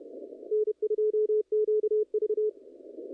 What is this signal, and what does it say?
Morse code, heard as a keyed tone, about 29 wpm, 430 Hz, with radio noise. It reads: N2QV
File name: contests / naqp-cw-k5tr-op-k5pi- -0100